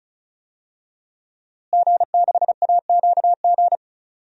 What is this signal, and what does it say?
Morse code, heard as a keyed tone, about 36 wpm, 700 Hz. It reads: G6AQG
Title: DXCC-100 Call Signs - 5 Characters - Repeated 3x - 35wpm